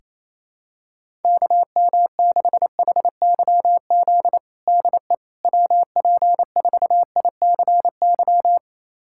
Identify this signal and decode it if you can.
Morse code, heard as a keyed tone, about 27 wpm, 700 Hz. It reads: KM6HYZ DE WP4ICY
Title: USA Callsign DE Callsign - 6 Characters - 28wpm